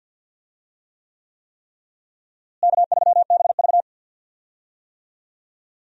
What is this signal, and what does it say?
Morse code, heard as a keyed tone, about 46 wpm, 700 Hz. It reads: K3BV
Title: USA Call Signs - 4 Characters - Repeated 3x - 50wpm